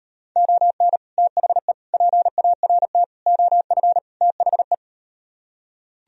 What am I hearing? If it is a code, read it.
Morse code, heard as a keyed tone, about 38 wpm, 700 Hz. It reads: ON THE PART OF THE